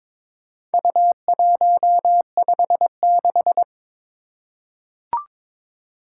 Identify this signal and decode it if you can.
Morse code, heard as a keyed tone, about 23 wpm, 700 Hz. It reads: U156
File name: practice / 22wpm / Four Letter-Number Combinations - 22wpm